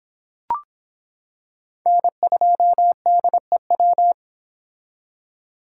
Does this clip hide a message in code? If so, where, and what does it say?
Yes: Morse code from 1.9 to 4.1 s, heard as a keyed tone, about 26 wpm, 700 Hz. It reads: N2DEW